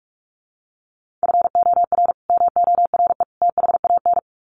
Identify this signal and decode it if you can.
Morse code, heard as a keyed tone, about 44 wpm, 700 Hz. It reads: FOR MORE THAN
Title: Sets of 3 Words 45wpm